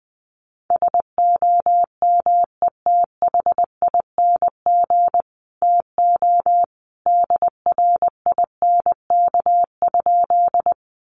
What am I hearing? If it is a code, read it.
Morse code, heard as a keyed tone, about 20 wpm, 700 Hz. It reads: SOMETHING TO DRINK?
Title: Sentences from Top 1500 Words 20wpm